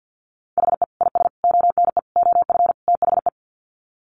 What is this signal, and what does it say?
Morse code, heard as a keyed tone, about 50 wpm, 700 Hz. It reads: HE IS ONE OF THE
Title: Sets of 5 Words Encore 50wpm